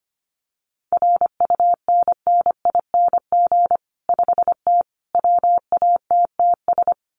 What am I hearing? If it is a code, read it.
Morse code, heard as a keyed tone, about 25 wpm, 700 Hz. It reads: RUNNING 5T WATTS
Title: Common QSO Phrases 25wpm